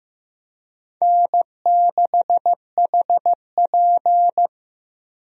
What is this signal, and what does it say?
Morse code, heard as a keyed tone, about 16 wpm, 700 Hz. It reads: N6HP
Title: USA Call Signs - 4 Characters - Repeated 2x - 15wpm